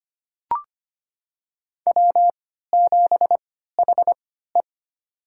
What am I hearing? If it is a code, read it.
Morse code, heard as a keyed tone, about 25 wpm, spaced as for 14 wpm, 700 Hz. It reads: W7HE
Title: USA Call Signs - 4 Characters - 3x Character Spacing - 25wpm